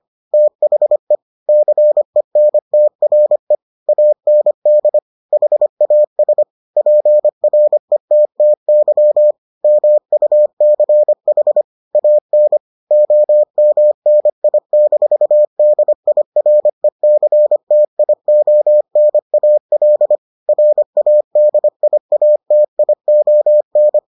Morse code, heard as a keyed tone, about 25 wpm, 600 Hz. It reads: THE CENTRE AND HAS PRETTY MUCH AN OMNI-DIRECTIONAL RADIATION